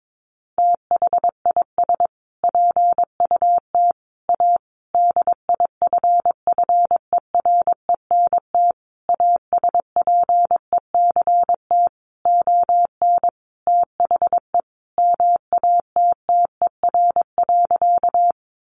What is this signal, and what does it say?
Morse code, heard as a keyed tone, about 22 wpm, 700 Hz. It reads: THIS PUT A DIFFERENT ASPECT ON THE MATTER.
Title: Sentences from Top 1300 Words 22wpm